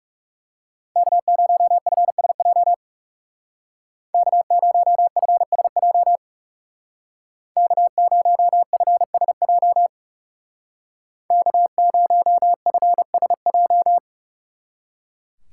Morse code, heard as a keyed tone, about 37 wpm, 700 Hz. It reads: K0FSJ K0FSJ K0FSJ K0FSJ
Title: Speed Racing - USA 1x3 Call Signs 30wpm